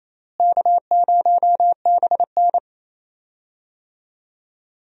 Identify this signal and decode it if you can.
Morse code, heard as a keyed tone, about 28 wpm, 700 Hz. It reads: K0BN
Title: USA Call Signs - 4 Characters - Repeated 3x - 28wpm